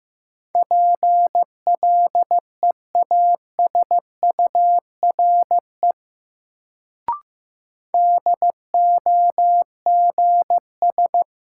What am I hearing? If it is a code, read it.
Morse code, heard as a keyed tone, about 15 wpm, 700 Hz. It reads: PLEASURE DOGS
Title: Sentences from Top 1600 Words - New Word Drill Alt - 15wpm